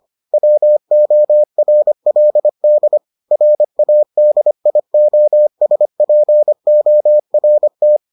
Morse code, heard as a keyed tone, about 25 wpm, 600 Hz. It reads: WORLD RADIOSPORT